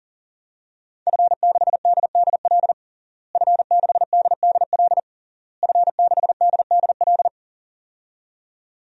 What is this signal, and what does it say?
Morse code, heard as a keyed tone, about 40 wpm, 700 Hz. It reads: F6DDL F6DDL F6DDL
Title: DXCC-100 Call Signs - 5 Characters - Repeated 3x - 40wpm